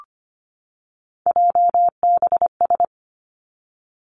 Morse code, heard as a keyed tone, about 25 wpm, 700 Hz. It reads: JBS
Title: Three Challenging Letter-Number Combinations 25wpm